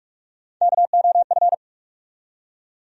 Morse code, heard as a keyed tone, about 44 wpm, 700 Hz. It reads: KOF
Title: USA Call Sign Postfixes - 3 Letters 45wpm